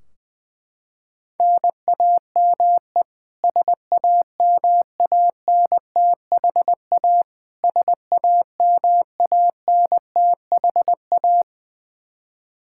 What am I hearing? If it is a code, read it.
Morse code, heard as a keyed tone, about 20 wpm, 700 Hz. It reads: NAME SAMANTHA SAMANTHA
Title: Practice QSO - Exchange 1 - Beginning - 20wpm